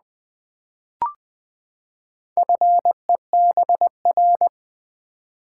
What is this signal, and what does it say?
Morse code, heard as a keyed tone, about 20 wpm, 700 Hz. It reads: FEBR
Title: Build A Word - 4 Letters At A Time - 8-Letter Words - 20wpm